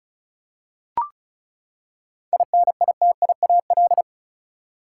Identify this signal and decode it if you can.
Morse code, heard as a keyed tone, about 34 wpm, 700 Hz. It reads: INITIAL
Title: Seven Letter Words - 35wpm